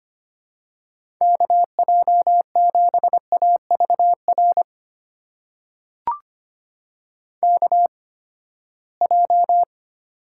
Morse code, heard as a keyed tone, about 25 wpm, spaced as for 17 wpm, 700 Hz. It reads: KJ7AVR KJ
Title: USA Call Signs - 6 Characters - 8x Character Spacing - 25wpm